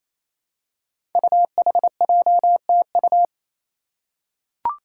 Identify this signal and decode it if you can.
Morse code, heard as a keyed tone, about 28 wpm, 700 Hz. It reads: UHJTU
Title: Five Challenging Letter-Number Combinations 28wpm